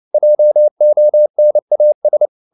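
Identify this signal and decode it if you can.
Morse code, heard as a keyed tone, about 29 wpm, 600 Hz. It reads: JONAS
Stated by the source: Fast (QRQ) Morse at 29 wpm, from CW med Gnister.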